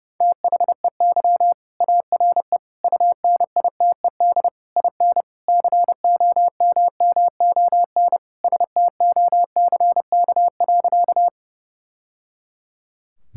Morse code, heard as a keyed tone, about 30 wpm, 700 Hz. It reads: THEY ARE UNITED IN COMMON STOCK.
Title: Sentences from Top 1000 Words - Review Part 5 - 30wpm